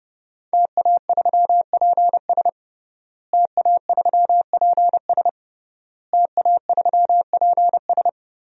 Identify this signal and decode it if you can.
Morse code, heard as a keyed tone, about 30 wpm, 700 Hz. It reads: TA3PS TA3PS TA3PS
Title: DXCC-100 Call Signs - 5 Characters - Repeated 3x - 30wpm